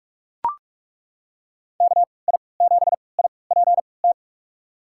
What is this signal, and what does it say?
Morse code, heard as a keyed tone, about 43 wpm, spaced as for 25 wpm, 700 Hz. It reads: KI7IPT
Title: USA Call Signs - 6 Characters - 3x Character Spacing - 45wpm